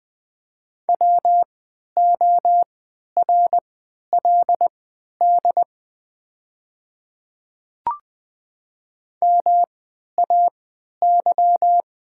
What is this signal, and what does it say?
Morse code, heard as a keyed tone, about 20 wpm, spaced as for 11 wpm, 700 Hz. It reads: WORLD MAY
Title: 3x Character Spacing - Top 300 Words - 20wpm